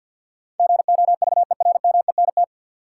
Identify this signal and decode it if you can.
Morse code, heard as a keyed tone, about 51 wpm, 700 Hz. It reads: GOVERMENT
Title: Sentences from Top 1300 Words - New Word Drill - 50wpm